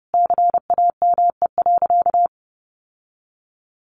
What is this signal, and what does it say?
Morse code, heard as a keyed tone, about 30 wpm, 700 Hz. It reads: CAME.
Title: Sentences from Top 1500 Words 30wpm